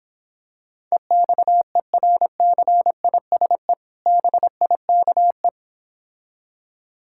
Morse code, heard as a keyed tone, about 26 wpm, 700 Hz. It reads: EXERCISE BIKE